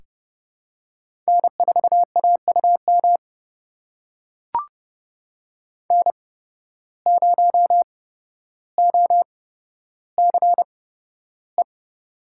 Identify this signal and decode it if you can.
Morse code, heard as a keyed tone, about 30 wpm, spaced as for 13 wpm, 700 Hz. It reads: N4AUM N0OCE
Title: USA Call Signs - 5 Characters - 8x Character Spacing - 30wpm